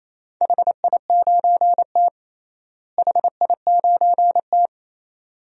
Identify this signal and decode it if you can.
Morse code, heard as a keyed tone, about 28 wpm, 700 Hz. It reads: HI9T HI9T
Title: DXCC-100 Call Signs - 4 Characters - Repeated 2x - 28wpm